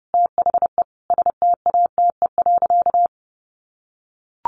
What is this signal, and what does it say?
Morse code, heard as a keyed tone, about 30 wpm, 700 Hz. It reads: THE STATE.
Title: Sentences from Top 1300 Words 30wpm